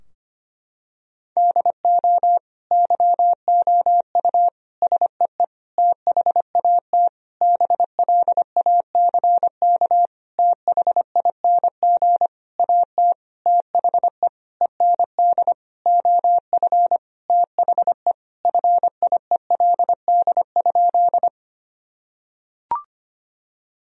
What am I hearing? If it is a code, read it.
Morse code, heard as a keyed tone, about 25 wpm, 700 Hz. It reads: DO YOU SEE THAT BLACK THING AT THE END OF THE FIELD?
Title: Sentences from Top 1000 Words - Review Part 2 - 25wpm